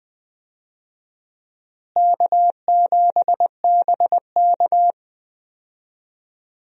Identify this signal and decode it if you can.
Morse code, heard as a keyed tone, about 20 wpm, 700 Hz. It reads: K7BK
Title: USA Call Signs - 4 Characters - Repeated 3x - 20wpm